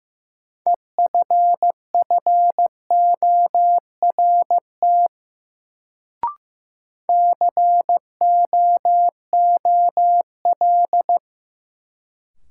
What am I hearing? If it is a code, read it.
Morse code, heard as a keyed tone, about 16 wpm, 700 Hz. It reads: EFFORT COOL
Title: Sentences from Top 1200 Words - New Word Drill Alt - 15wpm